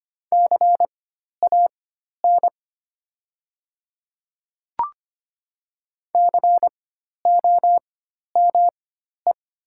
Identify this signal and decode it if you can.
Morse code, heard as a keyed tone, about 25 wpm, spaced as for 12 wpm, 700 Hz. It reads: CAN COME
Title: 4x Character Spacing - Top 100 Words - 25wpm